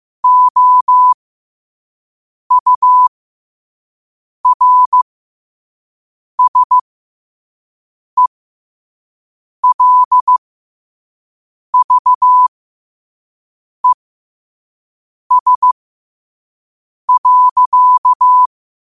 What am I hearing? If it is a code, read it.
Morse code, heard as a keyed tone, about 15 wpm, spaced as for 5 wpm, 1000 Hz. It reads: OURSELVES.
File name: qotdfiles / 5wpm